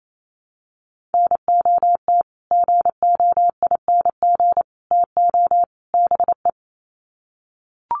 Morse code, heard as a keyed tone, about 28 wpm, 700 Hz. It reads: NOT GOING TO BE
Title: Sets of 4 Words 28wpm